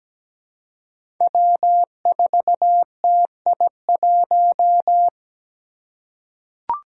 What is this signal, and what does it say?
Morse code, heard as a keyed tone, about 17 wpm, 700 Hz. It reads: W4TI1
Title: Five Challenging Letter-Number Combinations 17wpm